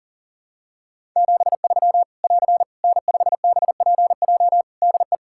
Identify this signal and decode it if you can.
Morse code, heard as a keyed tone, about 40 wpm, 700 Hz. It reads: 73 <AR> N5BPJ DE
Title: Practice QSO - Exchange 4 - Ending - 40wpm